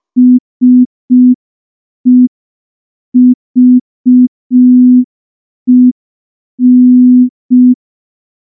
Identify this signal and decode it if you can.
Morse code, heard as a keyed tone, about 5 wpm, 250 Hz. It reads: SEVEN
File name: morse / morse-training-05WPM